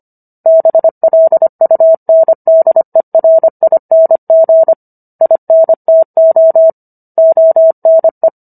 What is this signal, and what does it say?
Morse code, heard as a keyed tone, about 25 wpm, 650 Hz. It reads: BLUNDERING INTO ONE